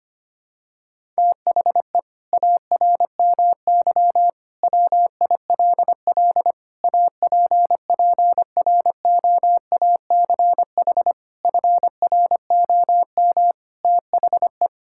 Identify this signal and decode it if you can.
Morse code, heard as a keyed tone, about 25 wpm, 700 Hz. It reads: THE ARMY WILL APPROACH FROM THE
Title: Sentences from Top 1000 Words - Review Part 6 - 25wpm